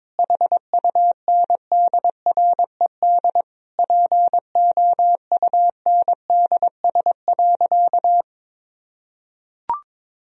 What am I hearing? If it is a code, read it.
Morse code, heard as a keyed tone, about 22 wpm, 700 Hz. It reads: HUNDRED POUNDS.
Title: Sentences from Top 1000 Words - Review Part 4 - 22wpm